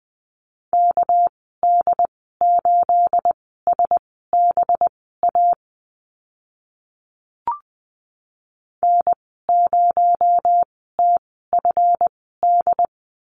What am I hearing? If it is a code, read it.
Morse code, heard as a keyed tone, about 20 wpm, spaced as for 14 wpm, 700 Hz. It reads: KD8SBA N0TFD
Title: 245 Lesson 39g - Letter z - Call Signs 20wpm